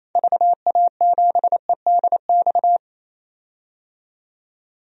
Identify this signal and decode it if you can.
Morse code, heard as a keyed tone, about 27 wpm, 700 Hz. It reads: VA7EDX
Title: DXCC-100 Call Signs - 6 Characters - 28wpm